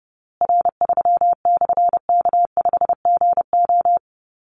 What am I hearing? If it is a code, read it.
Morse code, heard as a keyed tone, about 30 wpm, 700 Hz. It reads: R3/K5GO
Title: Call Signs Repeated 3x - 30wpm